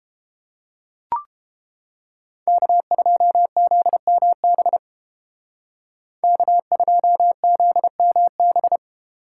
Morse code, heard as a keyed tone, about 32 wpm, 700 Hz. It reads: K2ZMB K2ZMB